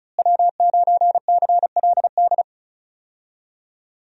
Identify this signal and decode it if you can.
Morse code, heard as a keyed tone, about 36 wpm, 700 Hz. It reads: W9CLD
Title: USA Call Signs - 5 Characters - Repeated 2x - 35wpm